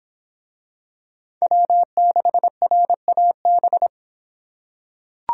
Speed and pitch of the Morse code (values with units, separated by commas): 26 wpm, 700 Hz